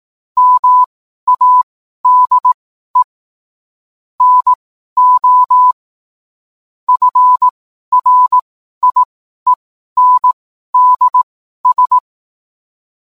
Morse code, heard as a keyed tone, about 18 wpm, spaced as for 12 wpm, 1000 Hz. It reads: MADE NO FRIENDS